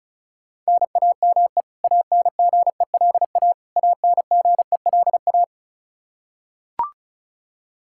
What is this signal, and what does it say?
Morse code, heard as a keyed tone, about 34 wpm, 700 Hz. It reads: NAME ANGELA ANGELA
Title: Practice QSO - Exchange 1 - Beginning - 35wpm